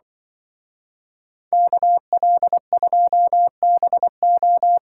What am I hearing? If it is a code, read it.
Morse code, heard as a keyed tone, about 23 wpm, 700 Hz. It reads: KL2BO
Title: Speed Racing - USA 2x2 Call Signs 20wpm